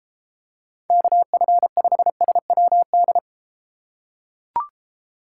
Morse code, heard as a keyed tone, about 33 wpm, 700 Hz. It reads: KF5SWD